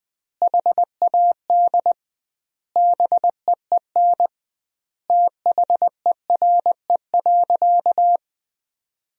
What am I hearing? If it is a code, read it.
Morse code, heard as a keyed tone, about 20 wpm, 700 Hz. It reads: HAD BEEN THERE.
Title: 2x Word-Spacing - Sentences from Top 300 Words - 20wpm